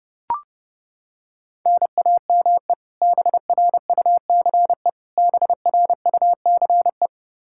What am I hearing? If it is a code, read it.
Morse code, heard as a keyed tone, about 30 wpm, 700 Hz. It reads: NAME BRUCE BRUCE
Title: Practice QSO - Exchange 1 - Beginning - 30wpm